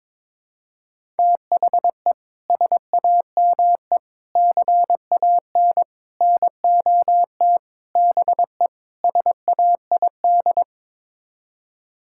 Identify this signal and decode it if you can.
Morse code, heard as a keyed tone, about 22 wpm, 700 Hz. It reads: THE SAME CAN NOT BE SAID
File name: practice / 22wpm / Sets of 6 Words - 22wpm